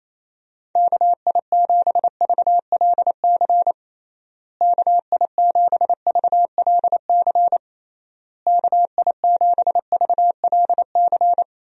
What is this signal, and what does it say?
Morse code, heard as a keyed tone, about 28 wpm, 700 Hz. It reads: KI7VLC KI7VLC KI7VLC
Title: USA Call Signs - 6 Characters - Repeated 3x - 28wpm